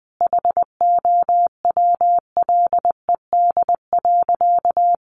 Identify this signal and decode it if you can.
Morse code, heard as a keyed tone, about 20 wpm, 700 Hz. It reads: HOWLED.
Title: A Sound Of Thunder 20wpm